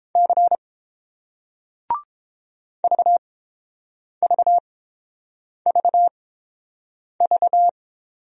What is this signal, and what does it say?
Morse code, heard as a keyed tone, about 30 wpm, spaced as for 7 wpm, 690 Hz. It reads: C VVVV